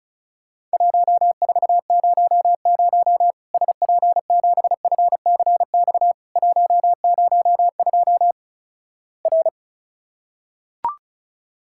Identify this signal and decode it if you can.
Morse code, heard as a keyed tone, about 36 wpm, 700 Hz. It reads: 1400 SP7FCX 102 R
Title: Worked All Europe DX Contest - QTC Exchange - 35wpm